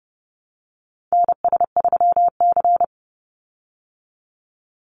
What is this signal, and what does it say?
Morse code, heard as a keyed tone, about 30 wpm, 700 Hz. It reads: NS3C
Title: Call Signs Repeated 2x - 30wpm